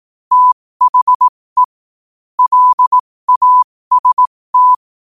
Morse code, heard as a keyed tone, about 18 wpm, spaced as for 15 wpm, 1000 Hz. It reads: THE LAST
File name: qotdfiles / 15wpm